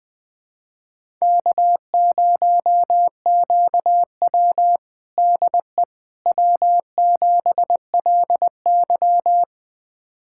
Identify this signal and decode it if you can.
Morse code, heard as a keyed tone, about 20 wpm, 700 Hz. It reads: K0QW DE W7LY
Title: USA Callsign DE Callsign - 4 Characters - 20wpm